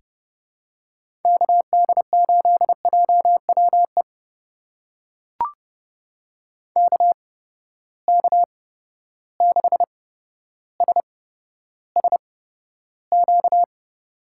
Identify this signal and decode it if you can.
Morse code, heard as a keyed tone, about 30 wpm, spaced as for 13 wpm, 700 Hz. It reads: KD8JWE KK6SSQ
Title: USA Call Signs - 6 Characters - 8x Character Spacing - 30wpm